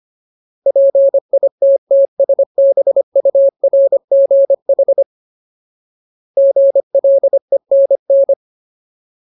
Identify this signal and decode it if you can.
Morse code, heard as a keyed tone, about 25 wpm, 550 Hz. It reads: PITTSBURGH GLENN
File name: Ultimate Word List / 25WPM Ultimate